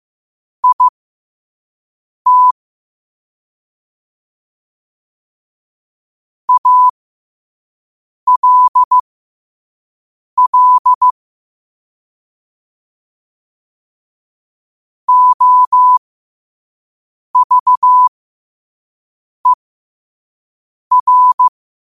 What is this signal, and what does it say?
Morse code, heard as a keyed tone, about 15 wpm, spaced as for 5 wpm, 1000 Hz. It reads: IT ALL OVER
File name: qotdfiles / 5wpm